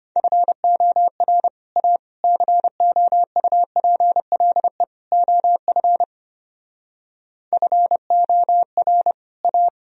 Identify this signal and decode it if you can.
Morse code, heard as a keyed tone, about 32 wpm, 700 Hz. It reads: FOR A COUPLE OF FOR A